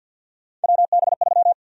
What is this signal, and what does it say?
Morse code, heard as a keyed tone, about 49 wpm, 700 Hz. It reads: WB3
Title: USA Call Sign Prefixes - 50wpm